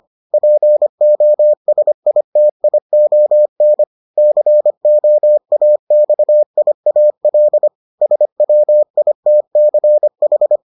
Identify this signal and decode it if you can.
Morse code, heard as a keyed tone, about 25 wpm, 600 Hz. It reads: POSITION COAXIAL SWITCH